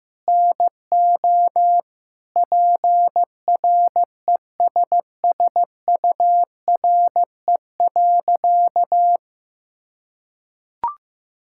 Morse code, heard as a keyed tone, about 15 wpm, 700 Hz. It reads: NO PRESSURE.
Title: Sentences from Top 1000 Words - Review Part 6 - 15wpm